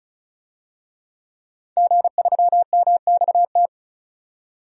Morse code, heard as a keyed tone, about 34 wpm, 700 Hz. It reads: G3MXT
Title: DXCC-100 Call Signs - 5 Characters - 35wpm